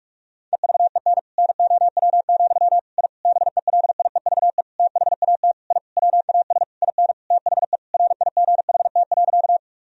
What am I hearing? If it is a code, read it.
Morse code, heard as a keyed tone, about 41 wpm, 700 Hz. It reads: EVEN NOW, I BELIEVE THAT I WAS IN THE RIGHT.
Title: Sentences from Top 1000 Words - Review Part 1 - 45wpm